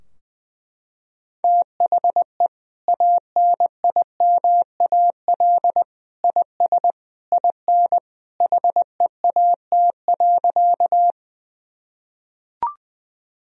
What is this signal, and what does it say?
Morse code, heard as a keyed tone, about 20 wpm, 700 Hz. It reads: THE ANIMAL IS IN HEAT.
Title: Sentences from Top 1000 Words - Review Part 3 - 20wpm